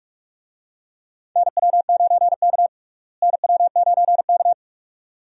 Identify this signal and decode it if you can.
Morse code, heard as a keyed tone, about 42 wpm, 700 Hz. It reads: NW9K NW9K
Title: USA Call Signs - 4 Characters - Repeated 2x - 45wpm